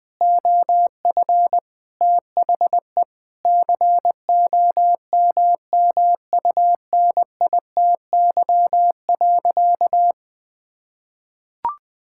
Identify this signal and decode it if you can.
Morse code, heard as a keyed tone, about 20 wpm, 700 Hz. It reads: OF THE COMMUNITY.
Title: Sentences with 6-Word Phrases - 20wpm